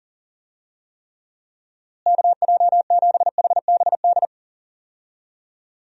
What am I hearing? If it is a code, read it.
Morse code, heard as a keyed tone, about 40 wpm, 700 Hz. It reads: KJ7HBD
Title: USA Call Signs - 6 Characters - Repeated 2x - 40wpm